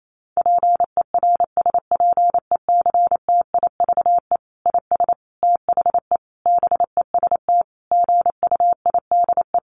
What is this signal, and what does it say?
Morse code, heard as a keyed tone, about 28 wpm, 700 Hz. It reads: PERSPECTIVE IS THE BEST GUIDE
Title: Sentences from Top 1200 Words 28wpm